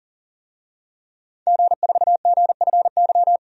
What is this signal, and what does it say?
Morse code, heard as a keyed tone, about 40 wpm, 700 Hz. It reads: G4GFY
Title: DXCC-100 Call Signs - 5 Characters - 40wpm